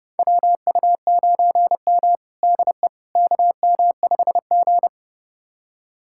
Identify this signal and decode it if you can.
Morse code, heard as a keyed tone, about 30 wpm, 700 Hz. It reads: WU9M DE KM5G
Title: USA Callsign DE Callsign - 4 Characters - 30wpm